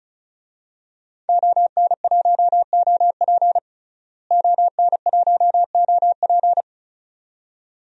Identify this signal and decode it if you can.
Morse code, heard as a keyed tone, about 36 wpm, 700 Hz. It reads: ON1OP ON1OP
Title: DXCC-100 Call Signs - 5 Characters - Repeated 2x - 35wpm